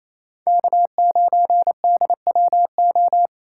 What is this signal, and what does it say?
Morse code, heard as a keyed tone, about 38 wpm, 700 Hz. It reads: K9DWO